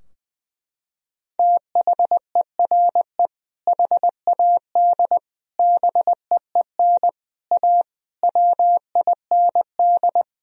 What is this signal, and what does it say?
Morse code, heard as a keyed tone, about 20 wpm, 700 Hz. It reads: THERE HAD BEEN A WIND
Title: Sentences from Top 1000 Words - Review Part 2 - 20wpm